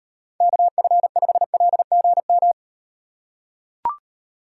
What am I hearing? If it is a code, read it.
Morse code, heard as a keyed tone, about 38 wpm, 700 Hz. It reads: KF5LGM